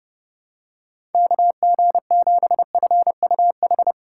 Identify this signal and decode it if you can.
Morse code, heard as a keyed tone, about 30 wpm, 700 Hz. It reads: KG7FUH
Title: USA Call Signs - 6 Characters - 2x Character Spacing - 30wpm